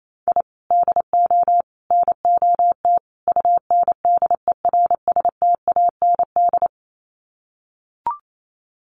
Morse code, heard as a keyed tone, about 28 wpm, 700 Hz. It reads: I DO NOT UNDERSTAND
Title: Sets of 4 Words 28wpm